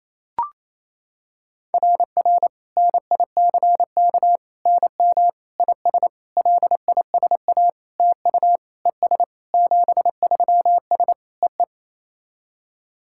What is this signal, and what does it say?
Morse code, heard as a keyed tone, about 28 wpm, 690 Hz. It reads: RR NICK NM IS LISA TU ES 73S E E
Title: SOTA QSO Practice 28wpm